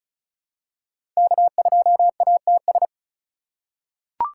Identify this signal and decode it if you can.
Morse code, heard as a keyed tone, about 34 wpm, 700 Hz. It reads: K2ATS
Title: USA Call Signs - 5 Characters - 3x Character Spacing - 35wpm